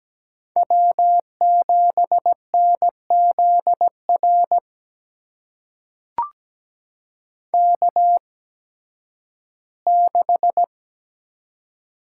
Morse code, heard as a keyed tone, about 17 wpm, spaced as for 11 wpm, 700 Hz. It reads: W7NZR K6
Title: USA Call Signs - 8x Character Spacing - 17wpm